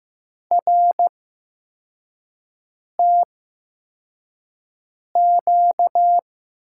Morse code, heard as a keyed tone, about 15 wpm, spaced as for 4 wpm, 700 Hz. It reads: RTQ
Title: USA Call Signs - 6 Characters - 8x Character Spacing - 15wpm